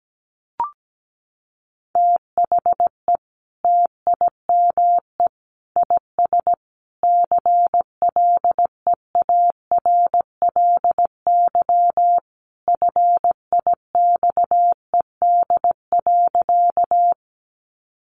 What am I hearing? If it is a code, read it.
Morse code, heard as a keyed tone, about 17 wpm, 700 Hz. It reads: THE TIME IS CLEARLY FIXED.
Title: Sentences from Top 1300 Words 17wpm